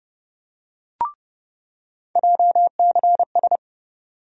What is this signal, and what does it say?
Morse code, heard as a keyed tone, about 30 wpm, 700 Hz. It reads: JCS
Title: USA Call Sign Postfixes - 3 Letters 30wpm